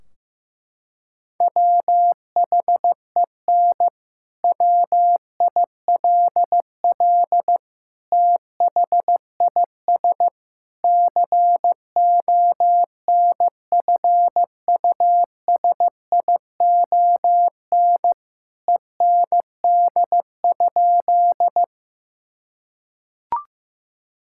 Morse code, heard as a keyed tone, about 14 wpm, 700 Hz. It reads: WHEN WILL THIS CONFUSION END?
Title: Sentences from Top 1600 Words 15wpm